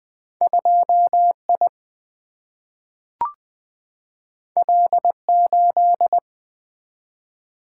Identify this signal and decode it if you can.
Morse code, heard as a keyed tone, about 20 wpm, 700 Hz. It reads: 2 I L 8
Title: Single-Letter Single-Number 20wpm